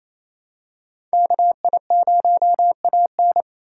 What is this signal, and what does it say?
Morse code, heard as a keyed tone, about 27 wpm, 700 Hz. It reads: KI0AN
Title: USA Call Signs - 5 Characters - 4x Character Spacing - 28wpm